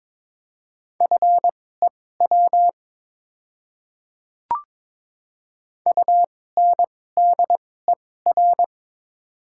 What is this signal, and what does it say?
Morse code, heard as a keyed tone, about 22 wpm, spaced as for 16 wpm, 690 Hz. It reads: FEW UNDER
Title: 2x Character Spacing - Top 500 Words - 22wpm